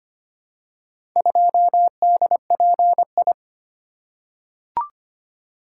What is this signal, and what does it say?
Morse code, heard as a keyed tone, about 25 wpm, 700 Hz. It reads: 2DPI
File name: practice / 25wpm / Four Letter-Number Combinations - 25wpm